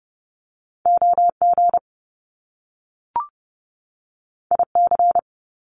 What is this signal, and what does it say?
Morse code, heard as a keyed tone, about 30 wpm, 700 Hz. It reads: OG IC
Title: Top Two Letter Combinations in Words 30wpm